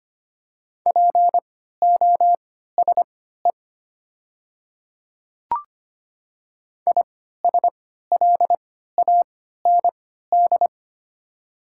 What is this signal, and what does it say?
Morse code, heard as a keyed tone, about 25 wpm, spaced as for 14 wpm, 700 Hz. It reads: POSE ISLAND
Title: 3x Character Spacing - Top 500 Words - 25wpm